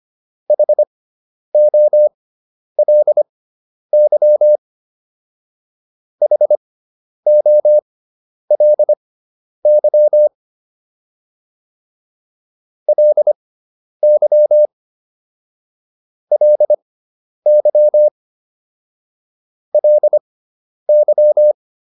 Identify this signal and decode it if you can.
Morse code, heard as a keyed tone, about 25 wpm, spaced as for 10 wpm, 600 Hz. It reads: HOLY HOLY LY LY LY